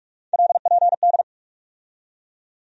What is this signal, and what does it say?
Morse code, heard as a keyed tone, about 46 wpm, 700 Hz. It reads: LPD
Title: USA Call Sign Postfixes - 3 Letters 45wpm